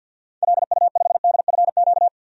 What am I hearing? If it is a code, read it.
Morse code, heard as a keyed tone, about 50 wpm, 700 Hz. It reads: LU5DFQ
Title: DXCC-100 Call Signs - 6 Characters - 50wpm